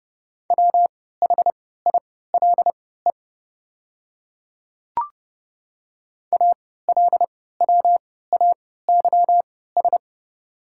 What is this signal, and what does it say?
Morse code, heard as a keyed tone, about 32 wpm, spaced as for 17 wpm, 700 Hz. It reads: WHILE ALWAYS
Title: 3x Character Spacing - Top 500 Words - 30wpm